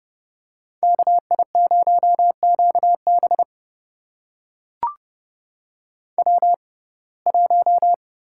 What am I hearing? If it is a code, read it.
Morse code, heard as a keyed tone, about 32 wpm, spaced as for 22 wpm, 700 Hz. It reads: KI0QB W1
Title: USA Call Signs - 5 Characters - 6x Character Spacing - 30wpm